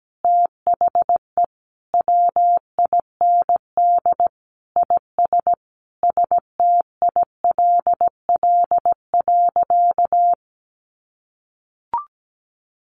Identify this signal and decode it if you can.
Morse code, heard as a keyed tone, about 17 wpm, 700 Hz. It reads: THE WIND IS STILL.
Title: Sentences from Top 1200 Words 17wpm